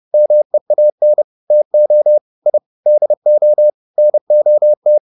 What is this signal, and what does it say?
Morse code, heard as a keyed tone, about 30 wpm, 600 Hz. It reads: MEAN TO I DO NOT